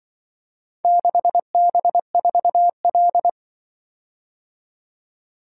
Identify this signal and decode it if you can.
Morse code, heard as a keyed tone, about 24 wpm, 700 Hz. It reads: 6B4L
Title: Speed-Racing - Four Letter-Number Combinations - 20wpm